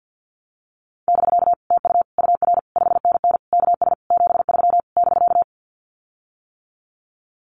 Morse code, heard as a keyed tone, about 50 wpm, 700 Hz. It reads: <BK> TU UR 5NN KS 73 <BK>
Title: Common QSO Phrases Encore 50wpm